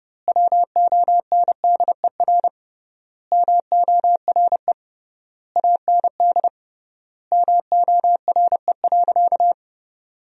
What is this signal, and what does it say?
Morse code, heard as a keyed tone, about 30 wpm, 700 Hz. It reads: WONDER MORE AND MORE.
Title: 3x Word-Spacing - Sentences from Top 500 Words - 30wpm